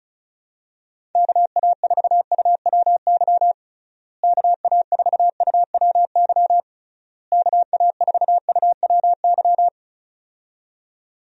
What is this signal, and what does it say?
Morse code, heard as a keyed tone, about 34 wpm, 700 Hz. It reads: KA4UWY KA4UWY KA4UWY
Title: USA Call Signs - 6 Characters - Repeated 3x - 35wpm